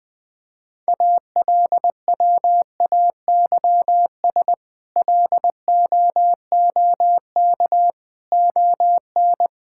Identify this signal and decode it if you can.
Morse code, heard as a keyed tone, about 20 wpm, 700 Hz. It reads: ALWAYS LOOK ON
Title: Sentences with 8-Word Phrases - 20wpm